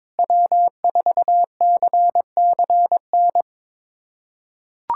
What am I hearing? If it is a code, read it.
Morse code, heard as a keyed tone, about 22 wpm, 700 Hz. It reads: W4CCN
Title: USA Callsign DE Callsign - 5 Characters - 22wpm